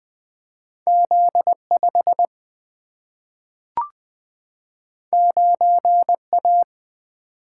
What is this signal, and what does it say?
Morse code, heard as a keyed tone, about 20 wpm, 700 Hz. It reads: Z 5 9 A
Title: Single-Letter Single-Number 20wpm